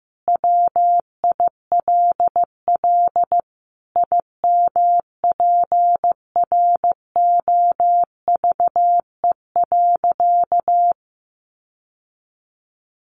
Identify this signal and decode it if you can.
Morse code, heard as a keyed tone, about 15 wpm, 700 Hz. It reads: WILL IMPROVE.
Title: Sentences from Top 800 Words 15wpm